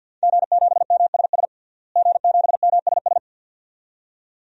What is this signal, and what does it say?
Morse code, heard as a keyed tone, about 50 wpm, 700 Hz. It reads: G7MSS G7MSS
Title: DXCC-100 Call Signs - 5 Characters - Repeated 2x - 50wpm